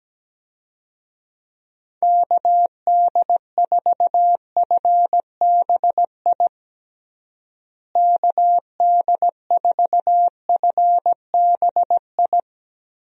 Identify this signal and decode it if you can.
Morse code, heard as a keyed tone, about 17 wpm, 700 Hz. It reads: KD4FBI KD4FBI
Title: USA Call Signs - 6 Characters - Repeated 3x - 17wpm